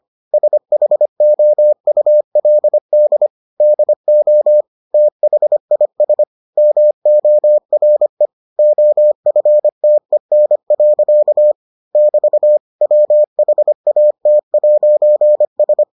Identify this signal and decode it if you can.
Morse code, heard as a keyed tone, about 25 wpm, 600 Hz. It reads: SHOULD DO THIS MORE OFTEN. <BT> WHAT'S